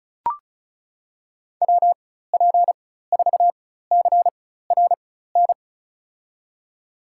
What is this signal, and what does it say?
Morse code, heard as a keyed tone, about 34 wpm, spaced as for 16 wpm, 700 Hz. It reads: WP4CRN
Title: USA Call Signs - 6 Characters - 4x Character Spacing - 35wpm